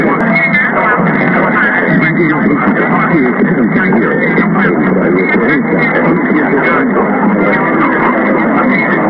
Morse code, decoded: none

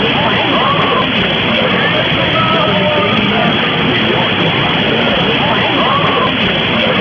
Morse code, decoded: none

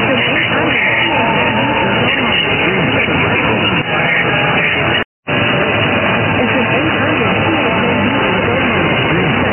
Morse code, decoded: none

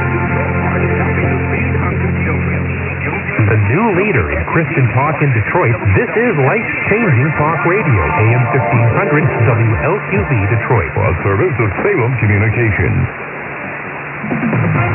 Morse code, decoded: none

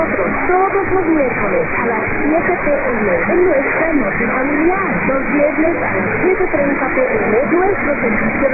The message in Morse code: none

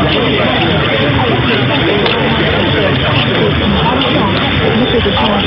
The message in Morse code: none

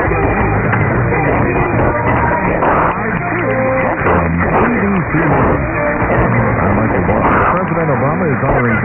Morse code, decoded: none

